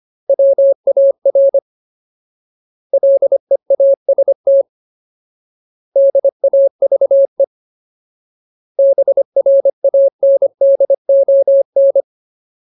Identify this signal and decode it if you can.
Morse code, heard as a keyed tone, about 25 wpm, 550 Hz. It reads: WAR LEAST DAVE BRANDON
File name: Ultimate Word List / 25WPM Ultimate